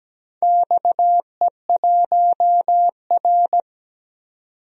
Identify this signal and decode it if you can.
Morse code, heard as a keyed tone, about 17 wpm, 700 Hz. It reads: XE1R
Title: DXCC-100 Call Signs - 4 Characters - 17wpm